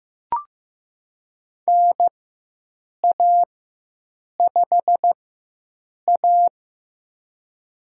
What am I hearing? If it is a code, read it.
Morse code, heard as a keyed tone, about 15 wpm, spaced as for 7 wpm, 700 Hz. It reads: NA5A
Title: USA Call Signs - 4x Character Spacing - 15wpm